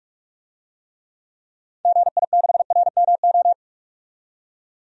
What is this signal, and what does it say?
Morse code, heard as a keyed tone, about 42 wpm, 700 Hz. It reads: GI6RMO
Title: DXCC-100 Call Signs - 6 Characters - 45wpm